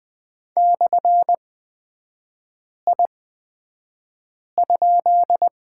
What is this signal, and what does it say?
Morse code, heard as a keyed tone, about 20 wpm, spaced as for 5 wpm, 700 Hz. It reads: /I?
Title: 243 Lesson 39e - Letter z - Three Characters 20wpm